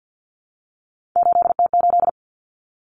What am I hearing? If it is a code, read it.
Morse code, heard as a keyed tone, about 50 wpm, 700 Hz. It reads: 8T8
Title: Common Cut Numbers - 3-Digit 50wpm